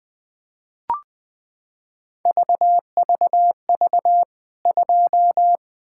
Morse code, heard as a keyed tone, about 20 wpm, 700 Hz. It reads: VVV 2
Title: K1USN SST QSO Practice 20wpm